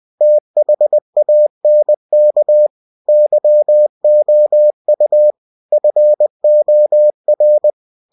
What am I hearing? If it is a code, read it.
Morse code, heard as a keyed tone, about 20 wpm, 600 Hz. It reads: THANK YOU FOR
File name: cw-books / winnie-the-pooh / winnie-20wpm-page-0088